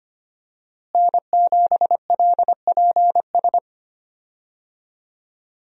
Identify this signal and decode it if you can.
Morse code, heard as a keyed tone, about 25 wpm, 700 Hz. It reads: N7LPS
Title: USA Call Signs - 5 Characters - Repeated 3x - 25wpm